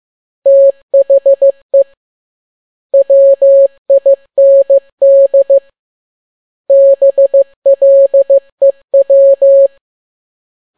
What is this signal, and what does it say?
Morse code, heard as a keyed tone, about 15 wpm, 550 Hz. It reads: THE WIND BLEW
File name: n6tt p1 15C13